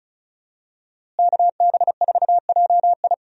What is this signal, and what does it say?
Morse code, heard as a keyed tone, about 36 wpm, 700 Hz. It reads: KB4JI
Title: USA Call Signs - 5 Characters - Repeated 2x - 35wpm